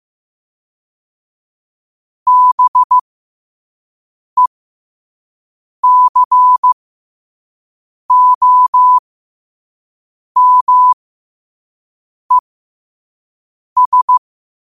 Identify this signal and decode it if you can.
Morse code, heard as a keyed tone, about 15 wpm, spaced as for 5 wpm, 1000 Hz. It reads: BECOMES
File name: qotdfiles / 5wpm